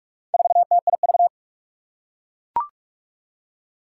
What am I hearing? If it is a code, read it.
Morse code, heard as a keyed tone, about 44 wpm, 700 Hz. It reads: 4TIV
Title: Four Challenging Letter-Number Combinations 45wpm